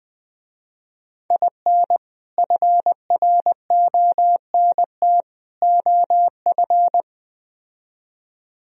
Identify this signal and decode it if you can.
Morse code, heard as a keyed tone, about 20 wpm, 700 Hz. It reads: IN FRONT OF
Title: Speed-Racing - Sets of 3 Words 17wpm